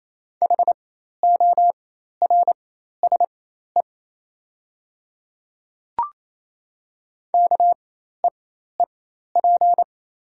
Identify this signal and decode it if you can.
Morse code, heard as a keyed tone, about 28 wpm, spaced as for 13 wpm, 700 Hz. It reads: HORSE KEEP
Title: 4x Character Spacing - Top 300 Words - 28wpm